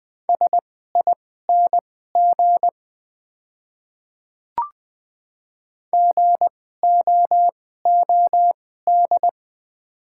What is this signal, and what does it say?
Morse code, heard as a keyed tone, about 20 wpm, spaced as for 14 wpm, 690 Hz. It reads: SING GOOD
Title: 2x Character Spacing - Top 500 Words - 20wpm